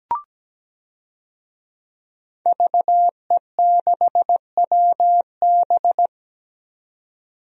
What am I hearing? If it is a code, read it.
Morse code, heard as a keyed tone, about 17 wpm, 700 Hz. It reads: VE6WB
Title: DXCC-100 Call Signs - 5 Characters - 17wpm